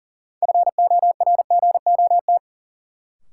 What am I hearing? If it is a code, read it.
Morse code, heard as a keyed tone, about 40 wpm, 700 Hz. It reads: FORGOT
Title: Sentences from Top 1400 Words - New Word Drill Alt - 40wpm